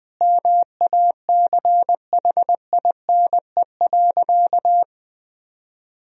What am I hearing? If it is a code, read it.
Morse code, heard as a keyed tone, about 20 wpm, 700 Hz. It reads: MACHINE.
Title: Sentences from Top 1000 Words - Review Part 2 - 20wpm